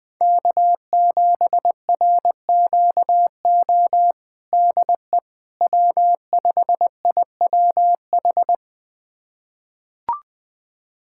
Morse code, heard as a keyed tone, about 20 wpm, 700 Hz. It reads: K7RQO DE W5IWH
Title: USA Callsign DE Callsign - 5 Characters - 20wpm